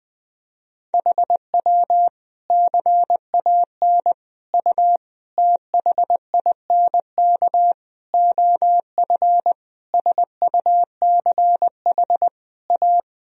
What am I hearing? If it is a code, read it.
Morse code, heard as a keyed tone, about 20 wpm, 700 Hz. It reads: HW CAN U THINK OF SUCH A
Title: Sentences from Top 200 Words and Abbreviations 20wpm